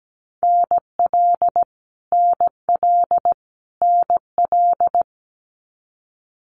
Morse code, heard as a keyed tone, about 17 wpm, 700 Hz. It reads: NL NL NL
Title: Canadian Provinces and Territories - Repeated 3x - 17wpm